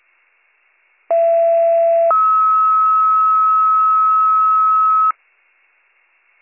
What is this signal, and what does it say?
Two-tone page, heard as a one-second tone, then a three-second tone, about 670 then 1290 Hz. It is heard from 1.1 to 5.1 s.